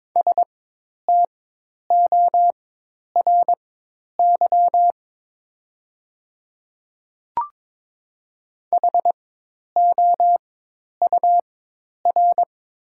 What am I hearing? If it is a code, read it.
Morse code, heard as a keyed tone, about 22 wpm, spaced as for 10 wpm, 700 Hz. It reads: STORY HOUR
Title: 4x Character Spacing - Top 500 Words - 22wpm